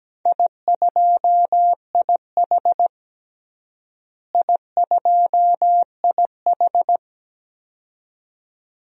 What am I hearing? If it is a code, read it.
Morse code, heard as a keyed tone, about 17 wpm, 700 Hz. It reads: I2IH I2IH
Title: DXCC-100 Call Signs - 4 Characters - Repeated 2x - 17wpm